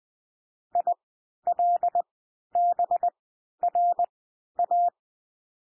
Morse code, heard as a keyed tone, about 20 wpm, spaced as for 11 wpm, 700 Hz. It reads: ILBRA
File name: cwmp3 / L1 SOUND LETTERS 20 wpm